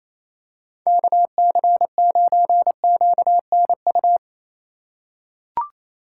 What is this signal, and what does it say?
Morse code, heard as a keyed tone, about 28 wpm, 700 Hz. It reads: KC9QNU
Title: USA Call Signs - 6 Characters - 2x Character Spacing - 28wpm